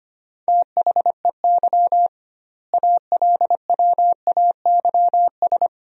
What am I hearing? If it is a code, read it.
Morse code, heard as a keyed tone, about 25 wpm, 700 Hz. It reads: THEY ALWAYS
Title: 2x Word-Spacing - Sentences from Top 300 Words - 25wpm